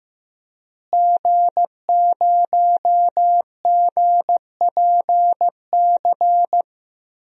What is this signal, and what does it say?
Morse code, heard as a keyed tone, about 15 wpm, 700 Hz. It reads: G0GPC
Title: DXCC-100 Call Signs - 5 Characters - Repeated 3x - 15wpm